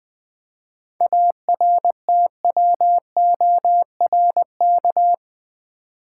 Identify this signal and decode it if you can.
Morse code, heard as a keyed tone, about 20 wpm, 700 Hz. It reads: ARTWORK
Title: 7 Letter Words - First 3 Letters Given - 20wpm